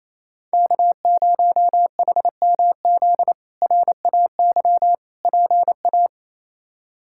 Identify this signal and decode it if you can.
Morse code, heard as a keyed tone, about 28 wpm, 700 Hz. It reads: K0HMZ RAY PA
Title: K1USN SST QSO Practice - 28wpm